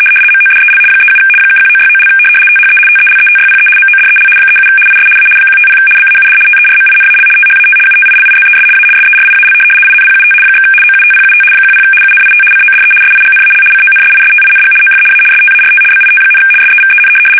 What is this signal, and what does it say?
RTTY, about 75 baud. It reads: MODEL 37 (1969) THE TELETYPE MODEL 37 IS A 150 BAUD TERMINAL THAT UTILIZES A SERIAL INPUT / OUTPUT 10 UNIT CODE SIGNAL CONSISTING OF A START BIT, SEVEN
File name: fsk850-75B / teletype-corp